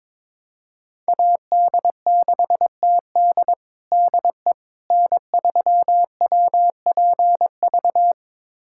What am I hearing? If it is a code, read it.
Morse code, heard as a keyed tone, about 21 wpm, 700 Hz. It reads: AD6TD DE N3WPV
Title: USA Callsign DE Callsign - 5 Characters - 22wpm